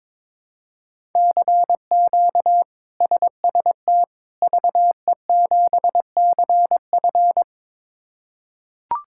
Morse code, heard as a keyed tone, about 22 wpm, 700 Hz. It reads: CQ SST VE7CF
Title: K1USN SST QSO Practice - 22wpm